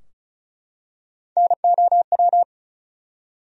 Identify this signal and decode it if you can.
Morse code, heard as a keyed tone, about 34 wpm, 700 Hz. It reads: NOW
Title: Top 100 Words Repeated 35wpm